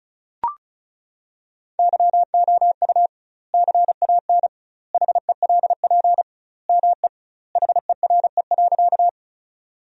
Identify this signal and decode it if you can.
Morse code, heard as a keyed tone, about 36 wpm, 700 Hz. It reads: YOU CAN HELP ME HERE.
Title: 2x Word-Spacing - Sentences from Top 200 Words - 35wpm